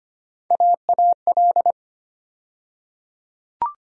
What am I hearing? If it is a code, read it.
Morse code, heard as a keyed tone, about 25 wpm, 700 Hz. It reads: AAL
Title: USA Call Sign Postfixes - 3 Letters 25wpm